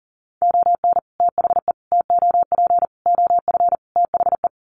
Morse code, heard as a keyed tone, about 40 wpm, 700 Hz. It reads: ON THE TOP OF THE
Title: Sets of 5 Words Encore 40wpm